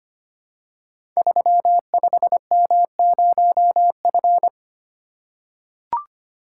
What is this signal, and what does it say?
Morse code, heard as a keyed tone, about 25 wpm, 700 Hz. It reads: 35M0F
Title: Code Groups - 5 Letter:Numbers - 25wpm